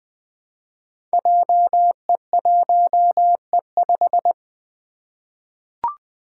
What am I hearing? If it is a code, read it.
Morse code, heard as a keyed tone, about 20 wpm, 700 Hz. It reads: JE1E5
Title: Five Challenging Letter-Number Combinations 20wpm